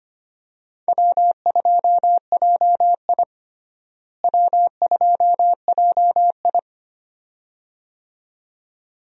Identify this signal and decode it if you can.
Morse code, heard as a keyed tone, about 25 wpm, 700 Hz. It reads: W2JI W2JI
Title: USA Call Signs - 4 Characters - Repeated 2x - 25wpm